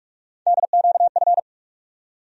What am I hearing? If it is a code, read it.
Morse code, heard as a keyed tone, about 45 wpm, 700 Hz. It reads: DQF
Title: USA Call Sign Postfixes - 3 Letters 45wpm